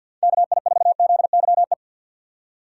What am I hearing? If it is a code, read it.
Morse code, heard as a keyed tone, about 52 wpm, 700 Hz. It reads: KI4ZYE
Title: USA Call Signs - 6 Characters - 4x Character Spacing - 50wpm